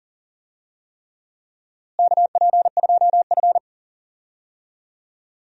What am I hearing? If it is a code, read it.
Morse code, heard as a keyed tone, about 39 wpm, 700 Hz. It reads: KP2F
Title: DXCC-100 Call Signs - 4 Characters - Repeated 3x - 40wpm